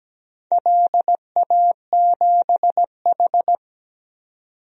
Morse code, heard as a keyed tone, about 17 wpm, 700 Hz. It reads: LA7H
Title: DXCC-100 Call Signs - 4 Characters - 17wpm